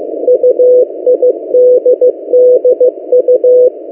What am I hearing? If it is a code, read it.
Morse code, heard as a keyed tone, about 15 wpm, 500 Hz. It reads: UIDDU
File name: M08a- -1355utc-8096khz